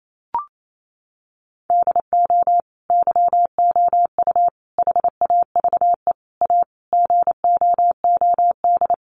Morse code, heard as a keyed tone, about 28 wpm, 700 Hz. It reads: DO YOU HAVE A GOOD
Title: Sentences from Top 1300 Words 28wpm